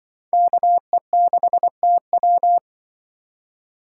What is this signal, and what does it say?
Morse code, heard as a keyed tone, about 25 wpm, 700 Hz. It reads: KE6TW
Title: Speed Racing - USA 2x2 Call Signs 20wpm